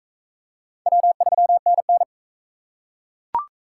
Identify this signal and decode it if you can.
Morse code, heard as a keyed tone, about 38 wpm, 700 Hz. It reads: W3NN